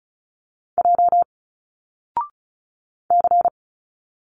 Morse code, heard as a keyed tone, about 36 wpm, 700 Hz. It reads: J C
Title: Single Letters 35wpm